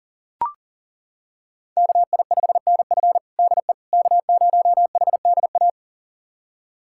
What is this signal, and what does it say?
Morse code, heard as a keyed tone, about 40 wpm, 700 Hz. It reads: KI5NF DE K0HDA
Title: USA Callsign DE Callsign - 5 Characters - 40wpm